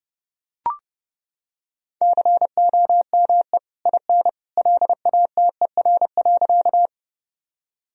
Morse code, heard as a keyed tone, about 29 wpm, 700 Hz. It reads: COME IN LATER.
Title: Sentences from Top 1000 Words - Review Part 4 - 30wpm